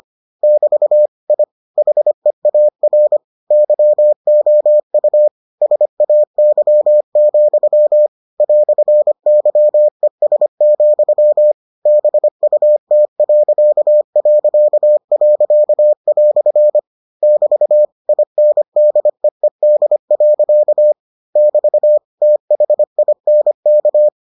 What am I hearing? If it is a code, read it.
Morse code, heard as a keyed tone, about 25 wpm, 600 Hz. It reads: <BT> I HEAR YOU SAY, "YES, BUT..." <BT> INDEED. <BT> THINK